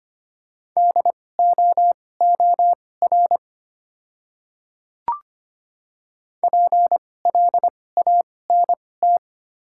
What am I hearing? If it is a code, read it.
Morse code, heard as a keyed tone, about 25 wpm, spaced as for 18 wpm, 690 Hz. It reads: DOOR PLANT
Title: 2x Character Spacing - Top 300 Words - 25wpm